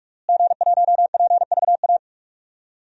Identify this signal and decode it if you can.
Morse code, heard as a keyed tone, about 46 wpm, 700 Hz. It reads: G1PVA
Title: DXCC-100 Call Signs - 5 Characters - Repeated 2x - 45wpm